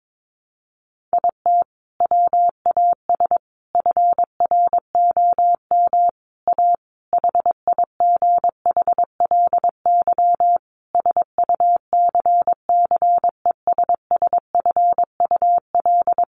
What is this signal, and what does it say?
Morse code, heard as a keyed tone, about 22 wpm, 700 Hz. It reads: IT WAS FROM A HIGHLY SUCCESSFUL
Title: Sentences from Top 1100 Words 22wpm